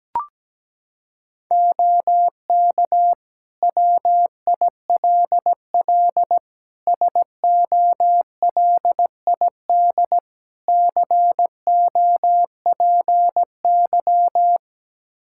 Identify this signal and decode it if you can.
Morse code, heard as a keyed tone, about 17 wpm, 700 Hz. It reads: OK WILL SOLID COPY
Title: Practice QSO - Exchange 3 - More Info - 17wpm